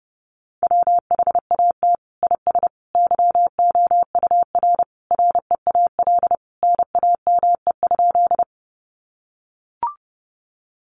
Morse code, heard as a keyed tone, about 28 wpm, 700 Hz. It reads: WHAT IS YOUR REAL NAME?
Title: Sentences from Top 300 Words 30wpm